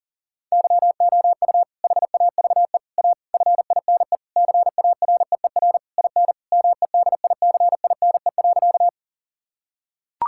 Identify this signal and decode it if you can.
Morse code, heard as a keyed tone, about 40 wpm, 700 Hz. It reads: YOU HAVE A FINE CAREER IN MEDICINE.
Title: Sentences from Top 1000 Words - Review Part 6 - 40wpm